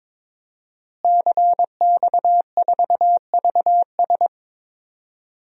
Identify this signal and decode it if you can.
Morse code, heard as a keyed tone, about 21 wpm, 700 Hz. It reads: CX4VS
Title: DXCC-100 Call Signs - 5 Characters - Repeated 3x - 22wpm